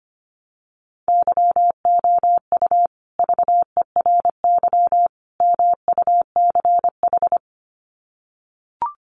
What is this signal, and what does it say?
Morse code, heard as a keyed tone, about 25 wpm, 700 Hz. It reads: YOU VERY MUCH
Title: Sets of 3 Words 25wpm